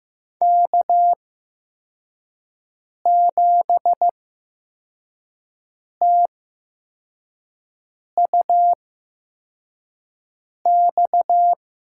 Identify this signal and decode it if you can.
Morse code, heard as a keyed tone, about 15 wpm, spaced as for 4 wpm, 700 Hz. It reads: K7TUX
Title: USA Call Signs - 5 Characters - 8x Character Spacing - 15wpm